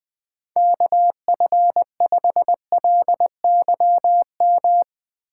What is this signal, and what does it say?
Morse code, heard as a keyed tone, about 20 wpm, 700 Hz. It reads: KF5LYM
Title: USA Call Signs - 6 Characters - 6x Character Spacing - 20wpm